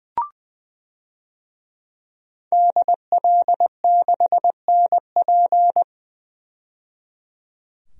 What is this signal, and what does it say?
Morse code, heard as a keyed tone, about 20 wpm, 700 Hz. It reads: DL6NP
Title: DXCC-100 Call Signs - 5 Characters - 20wpm